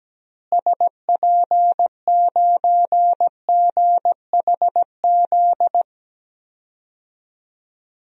Morse code, heard as a keyed tone, about 17 wpm, 700 Hz. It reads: SP9GHZ
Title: DXCC-100 Call Signs - 6 Characters - Repeated 2x - 17wpm